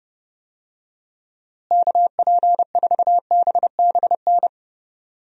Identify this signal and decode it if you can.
Morse code, heard as a keyed tone, about 32 wpm, 700 Hz. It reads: KP4BBN
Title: DXCC-100 Call Signs - 6 Characters - Repeated 2x - 30wpm